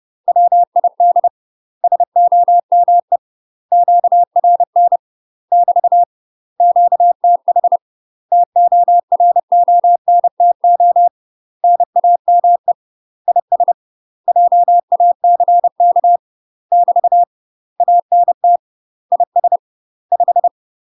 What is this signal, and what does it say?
Morse code, heard as a keyed tone, about 30 wpm, 700 Hz. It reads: WID SOME QRN <BT> QTH TORONTO NAME IS JACK <BT> ANT IS 5
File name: UK MASTER QSOs TRAINING / 30WPM / UK MASTER QSOs 0028